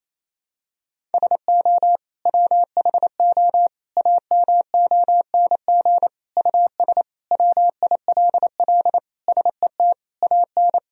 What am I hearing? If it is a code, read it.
Morse code, heard as a keyed tone, about 28 wpm, 700 Hz. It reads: SO WHO AMONG US WILL SET AN
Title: Sentences from Top 1000 Words - Review Part 3 - 28wpm